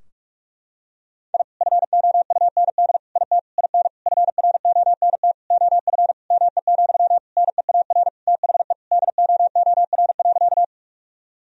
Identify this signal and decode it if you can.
Morse code, heard as a keyed tone, about 46 wpm, 700 Hz. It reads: I FOUND IT IN FRONT OF ME, NEAR THE DOOR.
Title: Sentences from Top 1000 Words - Review Part 2 - 45wpm